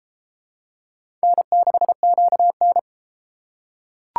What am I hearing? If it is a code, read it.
Morse code, heard as a keyed tone, about 32 wpm, 700 Hz. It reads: N6QN